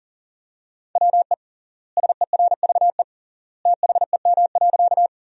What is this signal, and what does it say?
Morse code, heard as a keyed tone, about 40 wpm, 700 Hz. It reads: WE SERVE THEM.
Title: 3x Word-Spacing - Sentences from Top 400 Words - 40wpm